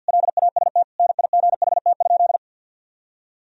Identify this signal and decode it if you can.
Morse code, heard as a keyed tone, about 52 wpm, 700 Hz. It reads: LAST NIGHT?
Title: Sentences from Top 300 Words and Abbreviations 50wpm